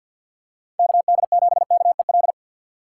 Morse code, heard as a keyed tone, about 50 wpm, 700 Hz. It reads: KD7KEL
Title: USA Call Signs - 6 Characters - 4x Character Spacing - 50wpm